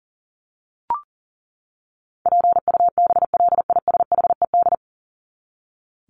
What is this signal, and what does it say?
Morse code, heard as a keyed tone, about 40 wpm, 700 Hz. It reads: PUBLISHED
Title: ED Suffix - Top 100 Words with Base Words 40wpm